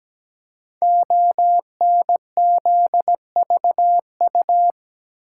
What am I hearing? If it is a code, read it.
Morse code, heard as a keyed tone, about 17 wpm, 700 Hz. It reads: ONZVU
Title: Code Groups - 5 Letter:Numbers - 17wpm